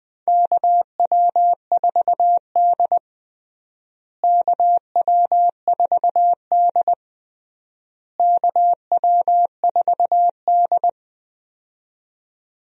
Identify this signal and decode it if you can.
Morse code, heard as a keyed tone, about 20 wpm, 700 Hz. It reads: KW4D KW4D KW4D
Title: USA Call Signs - 4 Characters - Repeated 3x - 20wpm